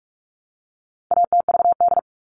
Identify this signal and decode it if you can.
Morse code, heard as a keyed tone, about 44 wpm, 700 Hz. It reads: ATVD
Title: All Cut Numbers - 4-Digit 45wpm